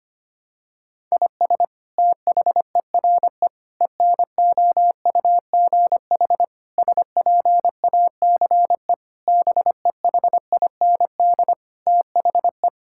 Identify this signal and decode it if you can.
Morse code, heard as a keyed tone, about 25 wpm, 700 Hz. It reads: IS THERE ENOUGH SPACE BEHIND THE
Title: Sentences from Top 1000 Words - Review Part 5 - 25wpm